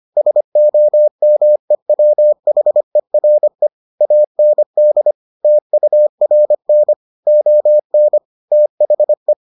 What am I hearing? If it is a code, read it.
Morse code, heard as a keyed tone, about 25 wpm, 600 Hz. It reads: SOMEWHERE AND TURN ON THE